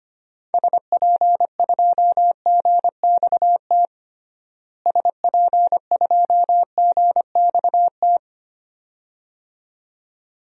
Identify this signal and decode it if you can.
Morse code, heard as a keyed tone, about 25 wpm, 700 Hz. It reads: SP2GXT SP2GXT
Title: DXCC-100 Call Signs - 6 Characters - Repeated 2x - 25wpm